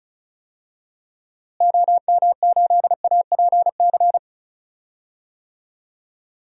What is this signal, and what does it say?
Morse code, heard as a keyed tone, about 36 wpm, 700 Hz. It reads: OM8APC
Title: DXCC-100 Call Signs - 6 Characters - Repeated 3x - 35wpm